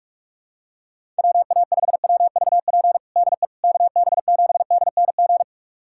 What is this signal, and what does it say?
Morse code, heard as a keyed tone, about 42 wpm, 700 Hz. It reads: WA5WVP DE KB7DNG
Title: USA Callsign DE Callsign - 6 Characters - 45wpm